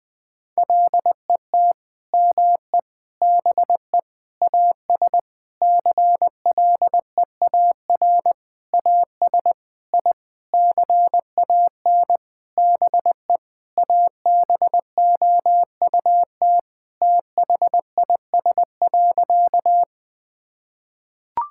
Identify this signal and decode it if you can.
Morse code, heard as a keyed tone, about 20 wpm, 700 Hz. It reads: LET ME BE AS CLEAR AS I CAN BE ABOUT THIS.
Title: Sentences with 9-Word Phrases - Spoken Beginning - 20wpm